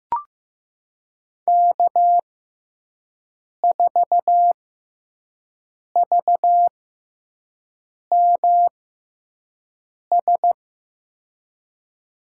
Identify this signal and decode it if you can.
Morse code, heard as a keyed tone, about 15 wpm, spaced as for 5 wpm, 700 Hz. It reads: K4VMS
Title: USA Call Signs - 5 Characters - 6x Character Spacing - 15wpm